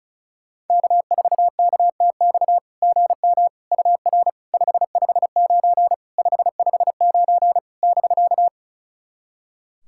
Morse code, heard as a keyed tone, about 36 wpm, 690 Hz. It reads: K4KTX GM UR 559 559 <BK>
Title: SOTA QSO Practice 35wpm